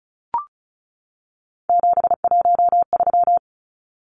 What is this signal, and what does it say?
Morse code, heard as a keyed tone, about 34 wpm, 700 Hz. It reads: 713
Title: Numbers - 3-Digit 35wpm